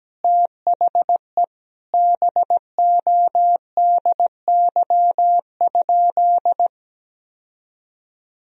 Morse code, heard as a keyed tone, about 17 wpm, 700 Hz. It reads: THE BODY?
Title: Sentences from Top 1000 Words - Review Part 2 - 17wpm